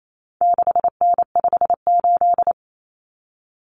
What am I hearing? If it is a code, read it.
Morse code, heard as a keyed tone, about 28 wpm, 700 Hz. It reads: 6N58
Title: Common Cut Numbers - 4-Digit 28wpm